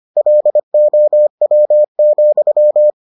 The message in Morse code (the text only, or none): LOW,